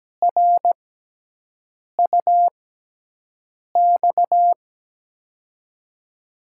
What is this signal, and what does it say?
Morse code, heard as a keyed tone, about 17 wpm, spaced as for 6 wpm, 700 Hz. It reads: RUX
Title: USA Call Signs - 6 Characters - 6x Character Spacing - 17wpm